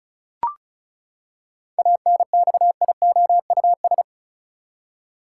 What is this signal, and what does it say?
Morse code, heard as a keyed tone, about 36 wpm, 700 Hz. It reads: ANXIOUS
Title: Seven Letter Words - 35wpm